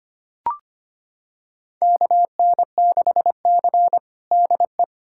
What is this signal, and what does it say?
Morse code, heard as a keyed tone, about 25 wpm, 700 Hz. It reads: KN6C DE
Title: USA Callsign DE Callsign - 4 Characters - 25wpm